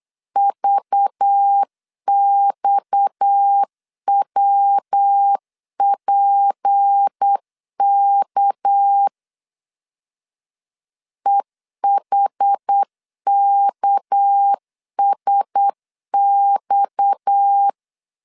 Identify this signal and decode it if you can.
Morse code, heard as a keyed tone, about 8 wpm, 800 Hz. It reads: VXWPK EHKSX
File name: CW / DET / 1 6 1500C S B 8wpm 5142